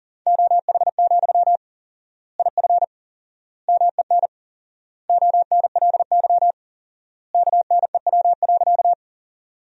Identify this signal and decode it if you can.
Morse code, heard as a keyed tone, about 40 wpm, 700 Hz. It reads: OH, IF MEN ONLY KNEW.
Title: 4x Word-Spacing - Sentences from Top 500 Words - 40wpm